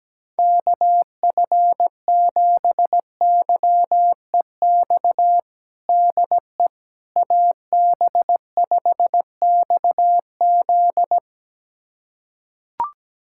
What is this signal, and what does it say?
Morse code, heard as a keyed tone, about 17 wpm, 700 Hz. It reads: KF7YEX DE AB5XZ
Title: USA Callsign DE Callsign 17wpm